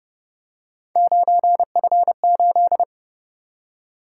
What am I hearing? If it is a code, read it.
Morse code, heard as a keyed tone, about 29 wpm, 700 Hz. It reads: 9F8